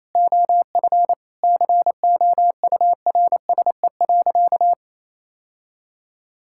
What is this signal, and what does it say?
Morse code, heard as a keyed tone, about 28 wpm, 700 Hz. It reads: OF COURSE.
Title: Sentences from Top 1600 Words 28wpm